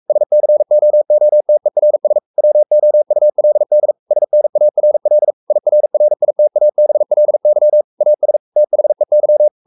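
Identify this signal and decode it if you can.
Morse code, heard as a keyed tone, about 44 wpm, 600 Hz. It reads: SCOOTERS WOULD SNARL IRRITABLY AS THEY
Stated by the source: Morse code at 43 wpm sending Ken Follett's novel Never in English, from CW med Gnister.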